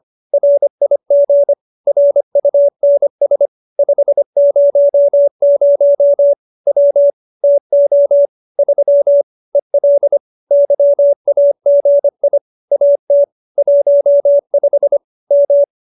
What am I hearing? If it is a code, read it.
Morse code, heard as a keyed tone, about 25 wpm, 570 Hz. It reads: RIG RUNS 500 W TO 3 EL YAGI AT 15 M